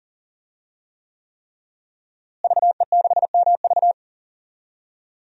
Morse code, heard as a keyed tone, about 40 wpm, 700 Hz. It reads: VE6MV
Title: DXCC-100 Call Signs - 5 Characters - Repeated 3x - 40wpm